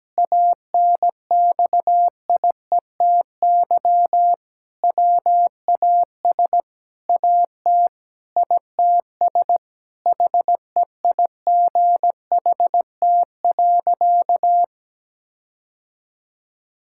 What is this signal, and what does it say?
Morse code, heard as a keyed tone, about 17 wpm, 700 Hz. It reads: ANXIETY WAS AT ITS HEIGHT.
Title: Sentences from Top 1600 Words 17wpm